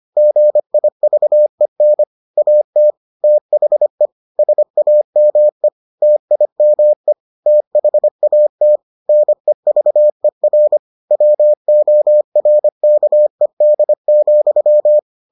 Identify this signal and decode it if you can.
Morse code, heard as a keyed tone, about 25 wpm, 600 Hz. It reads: GIVEN AT THE SAME TIME THAT NEVER WORKED,